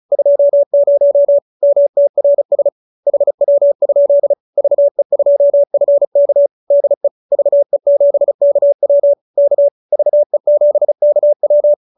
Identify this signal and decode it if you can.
Morse code, heard as a keyed tone, about 34 wpm, 570 Hz. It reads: 20 MTRS HW? VE2FK DE VE7KW K VE7KW
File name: QSO314 35